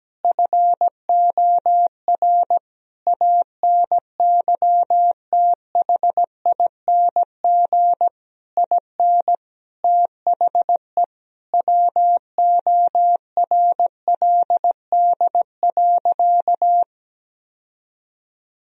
Morse code, heard as a keyed tone, about 17 wpm, 700 Hz. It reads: FOR ANYTHING IN THE WORLD.
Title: Sentences from Top 1000 Words - Review Part 3 - 17wpm